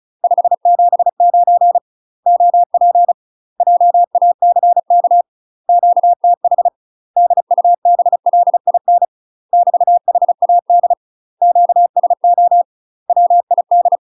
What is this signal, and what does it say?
Morse code, heard as a keyed tone, about 34 wpm, 700 Hz. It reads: 579 OP JACK QTH DUBLIN <BT> HAD QSO WID
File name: UK MASTER QSOs TRAINING / 35WPM / UK MASTER QSOs 0021